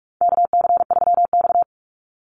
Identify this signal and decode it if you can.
Morse code, heard as a keyed tone, about 44 wpm, 700 Hz. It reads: KC3X
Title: Call Signs Repeated 2x - 45wpm